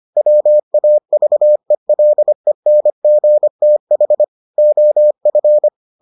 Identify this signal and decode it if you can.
Morse code, heard as a keyed tone, about 25 wpm, 600 Hz. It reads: WAVELENGTH OF